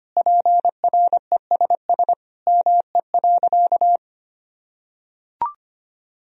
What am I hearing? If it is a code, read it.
Morse code, heard as a keyed tone, about 25 wpm, 700 Hz. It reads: PRESS ME.
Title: Sentences from Top 1000 Words - Review Part 1 - 25wpm